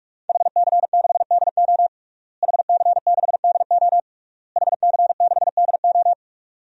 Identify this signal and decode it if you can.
Morse code, heard as a keyed tone, about 46 wpm, 700 Hz. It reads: HC6DO HC6DO HC6DO
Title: DXCC-100 Call Signs - 5 Characters - Repeated 3x - 45wpm